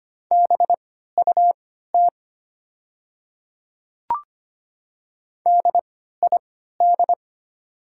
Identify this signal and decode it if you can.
Morse code, heard as a keyed tone, about 25 wpm, spaced as for 14 wpm, 700 Hz. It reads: BUT DID
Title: 3x Character Spacing - Top 100 Words - 25wpm